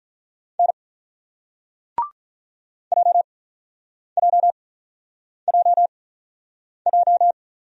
Morse code, heard as a keyed tone, about 48 wpm, spaced as for 9 wpm, 700 Hz. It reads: N JJJJ